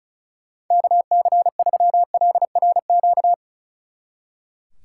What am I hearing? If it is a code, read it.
Morse code, heard as a keyed tone, about 33 wpm, 700 Hz. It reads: KC3LRQ
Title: Speed Racing - USA 2x3 Call Signs 35wpm